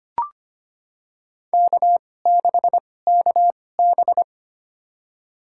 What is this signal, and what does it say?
Morse code, heard as a keyed tone, about 25 wpm, spaced as for 18 wpm, 700 Hz. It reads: K6KB
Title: USA Call Signs - 2x Character Spacing - 25wpm